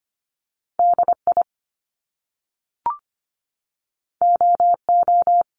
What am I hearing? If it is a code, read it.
Morse code, heard as a keyed tone, about 25 wpm, 700 Hz. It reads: DI OO
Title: Top Two Letter Combinations in Words 25wpm